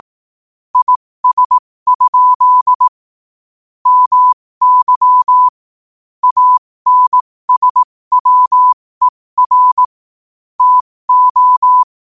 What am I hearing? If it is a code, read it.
Morse code, heard as a keyed tone, about 18 wpm, spaced as for 15 wpm, 1000 Hz. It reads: IS? MY ANSWER TO
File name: qotdfiles / 15wpm